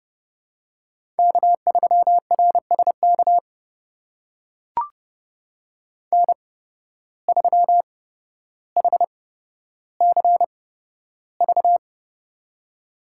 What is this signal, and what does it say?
Morse code, heard as a keyed tone, about 30 wpm, spaced as for 13 wpm, 700 Hz. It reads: K3RSK N3HCV
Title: USA Call Signs - 5 Characters - 8x Character Spacing - 30wpm